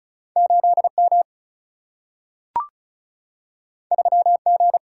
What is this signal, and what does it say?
Morse code, heard as a keyed tone, about 36 wpm, 700 Hz. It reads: 8 M 3 G
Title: Single-Letter Single-Number 35wpm